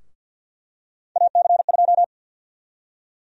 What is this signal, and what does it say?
Morse code, heard as a keyed tone, about 52 wpm, 700 Hz. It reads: AC2
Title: USA Call Sign Prefixes - Repeated 2x - 50wpm